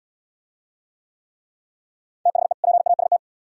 Morse code, heard as a keyed tone, about 70 wpm, 700 Hz. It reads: THE PAST